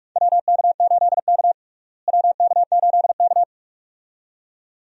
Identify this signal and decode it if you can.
Morse code, heard as a keyed tone, about 46 wpm, 700 Hz. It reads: WK8K WK8K
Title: USA Call Signs - 4 Characters - Repeated 2x - 45wpm